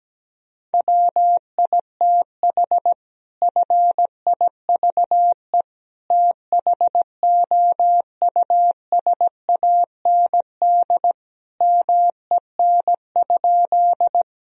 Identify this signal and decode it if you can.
Morse code, heard as a keyed tone, about 17 wpm, 700 Hz. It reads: WITH FIVE THOUSAND MEN?
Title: Sentences from Top 1000 Words - Review Part 3 - 17wpm